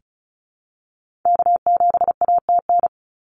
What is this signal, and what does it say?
Morse code, heard as a keyed tone, about 34 wpm, 700 Hz. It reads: K7ATN
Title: NA SOTA Activator Callsigns 35wpm